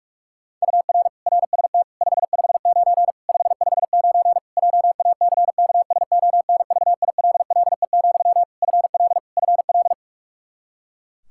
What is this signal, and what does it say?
Morse code, heard as a keyed tone, about 41 wpm, 700 Hz. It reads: UR RST 559 559 JACKSONVILLE, FL FL
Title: Practice QSO - Exchange 1 - Beginning - 45wpm